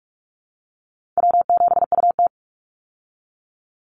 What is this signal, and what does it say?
Morse code, heard as a keyed tone, about 44 wpm, 700 Hz. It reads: W7UT
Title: Easy Call Signs Repeated 3x - 45wpm